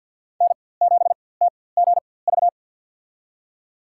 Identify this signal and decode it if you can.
Morse code, heard as a keyed tone, about 47 wpm, spaced as for 23 wpm, 700 Hz. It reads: N7TGV
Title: USA Call Signs - 4x Character Spacing - 50wpm